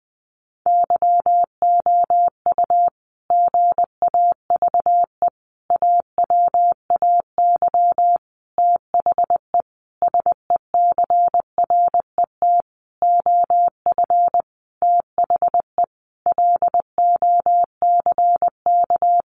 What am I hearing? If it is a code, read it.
Morse code, heard as a keyed tone, about 20 wpm, 700 Hz. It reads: YOU GAVE AWAY THE SECRET OF THE LOCK
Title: Sentences from Top 1300 Words 20wpm